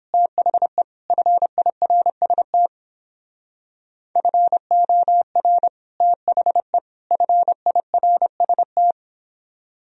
Morse code, heard as a keyed tone, about 27 wpm, 700 Hz. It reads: THE FIRST FOR THE FIRST
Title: Speed-Racing - Sets of 3 Words 22wpm